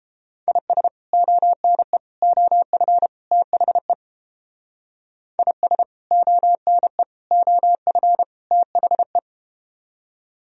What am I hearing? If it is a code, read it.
Morse code, heard as a keyed tone, about 30 wpm, 700 Hz. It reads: IS ONE OF THE IS ONE OF THE